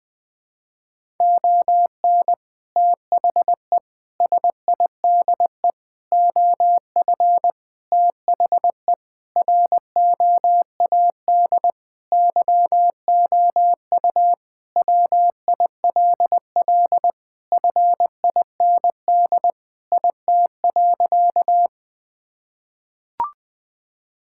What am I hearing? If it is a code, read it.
Morse code, heard as a keyed tone, about 21 wpm, 700 Hz. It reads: ON THE SIDE OF THE ROAD YOU WILL FIND IT.
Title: Sentences with 3-Word Phrases - 20wpm